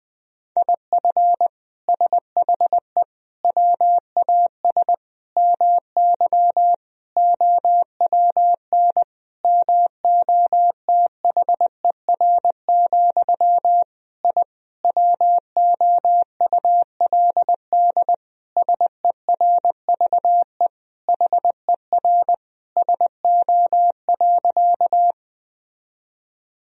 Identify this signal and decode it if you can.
Morse code, heard as a keyed tone, about 20 wpm, 700 Hz. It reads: IF SHE WAS MY OWN MOTHER, I WOULD SERVE HER SO.
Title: Sentences from Top 1000 Words - Review Part 2 - 20wpm